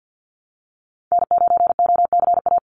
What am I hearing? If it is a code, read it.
Morse code, heard as a keyed tone, about 50 wpm, 700 Hz. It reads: N9OCA
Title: USA Call Signs 50wpm